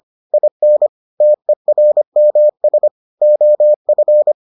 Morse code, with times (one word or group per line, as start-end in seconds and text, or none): 0.3-0.9 IN
1.2-2.9 TERMS
3.2-4.3 OF